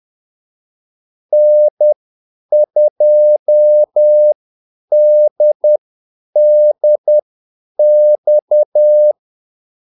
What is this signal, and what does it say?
Morse code, heard as a keyed tone, about 10 wpm, spaced as for 8 wpm, 600 Hz. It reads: N2DDX
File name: Callsigns-10-8wpm